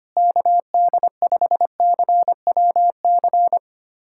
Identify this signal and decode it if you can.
Morse code, heard as a keyed tone, about 25 wpm, 700 Hz. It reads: KD5CWC
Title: USA Callsign DE Callsign 25wpm